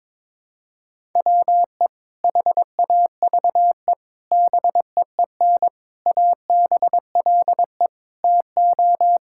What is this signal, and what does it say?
Morse code, heard as a keyed tone, about 22 wpm, 700 Hz. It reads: WE HAVE BEEN ABLE TO
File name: practice / 22wpm / Sentences with 4-Word Phrases - 22wpm